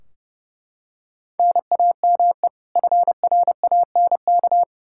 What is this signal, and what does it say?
Morse code, heard as a keyed tone, about 32 wpm, 700 Hz. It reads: NAME FRANK
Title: Practice QSO - Exchange 1 - Beginning - 30wpm